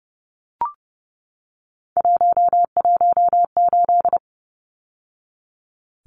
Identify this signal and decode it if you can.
Morse code, heard as a keyed tone, about 30 wpm, 700 Hz. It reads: 118
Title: Numbers - 3-Digit 30wpm